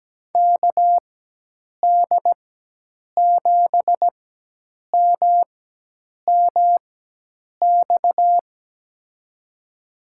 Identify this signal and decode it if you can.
Morse code, heard as a keyed tone, about 17 wpm, spaced as for 8 wpm, 700 Hz. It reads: KD7MMX
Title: USA Call Signs - 6 Characters - 4x Character Spacing - 17wpm